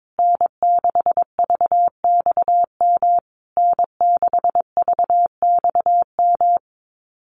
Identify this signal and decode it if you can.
Morse code, heard as a keyed tone, about 22 wpm, 700 Hz. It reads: N6VXM N6VXM
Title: USA Call Signs Repeated 3x 22wpm